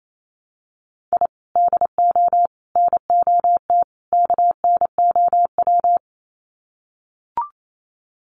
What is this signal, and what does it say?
Morse code, heard as a keyed tone, about 28 wpm, 700 Hz. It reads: I DO NOT KNOW
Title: Sets of 4 Words 28wpm